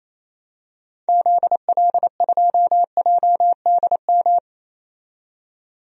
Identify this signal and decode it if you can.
Morse code, heard as a keyed tone, about 28 wpm, 700 Hz. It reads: ZL2JDM
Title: DXCC-100 Call Signs - 6 Characters - 28wpm